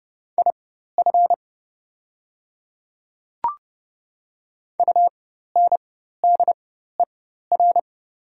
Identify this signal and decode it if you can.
Morse code, heard as a keyed tone, about 30 wpm, spaced as for 14 wpm, 690 Hz. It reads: IF UNDER
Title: 4x Character Spacing - Top 500 Words - 30wpm